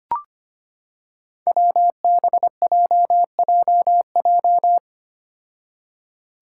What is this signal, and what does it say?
Morse code, heard as a keyed tone, about 25 wpm, 700 Hz. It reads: WBJJJ
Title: Five Challenging Letter-Number Combinations 25wpm